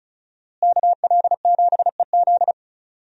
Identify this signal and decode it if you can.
Morse code, heard as a keyed tone, about 36 wpm, 700 Hz. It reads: KL7EZ
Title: DXCC-100 Call Signs - 5 Characters - Repeated 3x - 35wpm